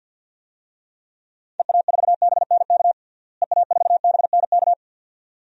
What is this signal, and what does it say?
Morse code, heard as a keyed tone, about 52 wpm, 700 Hz. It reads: EA4BNK EA4BNK
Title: DXCC-100 Call Signs - 6 Characters - Repeated 2x - 50wpm